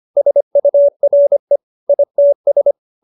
Morse code, heard as a keyed tone, about 25 wpm, 570 Hz. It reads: SURE ITS
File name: QSO312 25